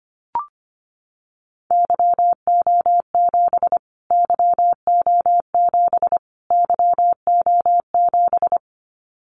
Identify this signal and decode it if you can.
Morse code, heard as a keyed tone, about 25 wpm, 700 Hz. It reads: YO7 YO7 YO7
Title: Callsign Prefixes for Top 25 DXCC - Repeated 3x - 25wpm